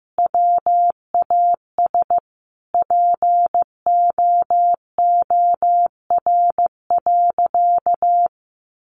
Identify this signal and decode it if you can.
Morse code, heard as a keyed tone, about 14 wpm, 700 Hz. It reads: WAS POOR.
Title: Sentences from Top 1000 Words 15wpm